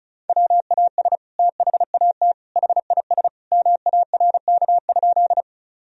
Morse code, heard as a keyed tone, about 34 wpm, 700 Hz. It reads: WAS THAT HIS MARK?
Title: Sentences from Top 1000 Words - Review Part 1 - 35wpm